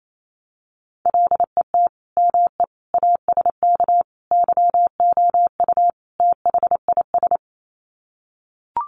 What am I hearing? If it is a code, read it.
Morse code, heard as a keyed tone, about 28 wpm, 700 Hz. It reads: LET ME ASK YOU THIS
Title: Sets of 5 Words 28wpm